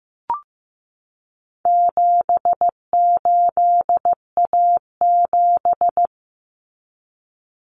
Morse code, heard as a keyed tone, about 15 wpm, 700 Hz. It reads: 78A7
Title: Common Cut Numbers - 4-Digit 15wpm